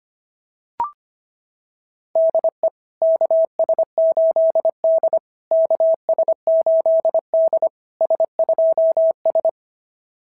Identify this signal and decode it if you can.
Morse code, heard as a keyed tone, about 24 wpm, 650 Hz. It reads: DE KS8D KS8D S2S
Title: SOTA QSO Practice 25wpm